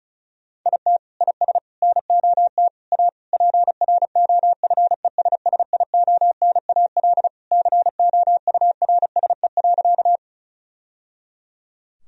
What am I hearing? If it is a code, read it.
Morse code, heard as a keyed tone, about 36 wpm, 700 Hz. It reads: IT IS NOT A PROFESSIONAL COURSE.
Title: Sentences from Top 1000 Words - Review Part 4 - 35wpm